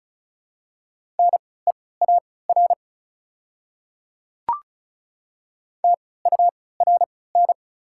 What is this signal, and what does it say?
Morse code, heard as a keyed tone, about 36 wpm, spaced as for 20 wpm, 700 Hz. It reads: NEAR TURN
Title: 3x Character Spacing - Top 200 Words - 35wpm